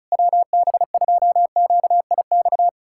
Morse code, heard as a keyed tone, about 38 wpm, 700 Hz. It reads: WB2QIX
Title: Speed-Racing Common QSO Phrases - 30wpm